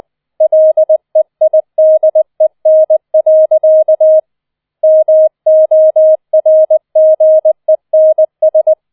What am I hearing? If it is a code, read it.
Morse code, heard as a keyed tone, about 20 wpm, 630 Hz. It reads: LEIDEN. MORGENS